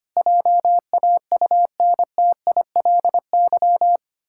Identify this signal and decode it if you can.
Morse code, heard as a keyed tone, about 25 wpm, 700 Hz. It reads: JAUNTILY
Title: Pangram Challenge v2 - 25wpm